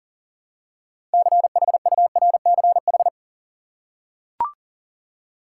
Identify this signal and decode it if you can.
Morse code, heard as a keyed tone, about 40 wpm, 700 Hz. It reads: CHURCH
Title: Six Letter Words - 40wpm